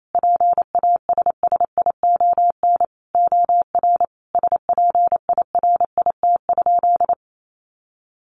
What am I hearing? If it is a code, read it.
Morse code, heard as a keyed tone, about 28 wpm, 700 Hz. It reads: PASSION OR SPIRIT?
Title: Sentences from Top 1400 Words 28wpm